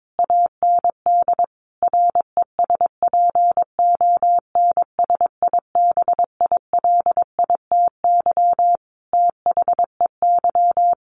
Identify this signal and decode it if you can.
Morse code, heard as a keyed tone, about 22 wpm, 700 Hz. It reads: AND RESPONSIBILITY THEY
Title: Sentences from Top 1200 Words 22wpm